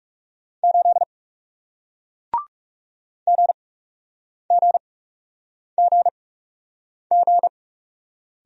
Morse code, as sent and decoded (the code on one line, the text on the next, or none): ---.. / --. --. --. --.
8 GGGG